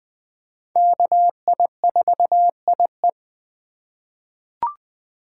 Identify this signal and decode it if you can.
Morse code, heard as a keyed tone, about 20 wpm, 700 Hz. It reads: KI4IE
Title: USA Call Signs - 5 Characters - 2x Character Spacing - 20wpm